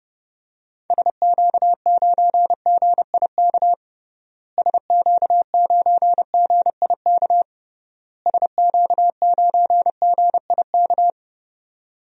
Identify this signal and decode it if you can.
Morse code, heard as a keyed tone, about 30 wpm, 700 Hz. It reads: SQ9GIK SQ9GIK SQ9GIK
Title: DXCC-100 Call Signs - 6 Characters - Repeated 3x - 30wpm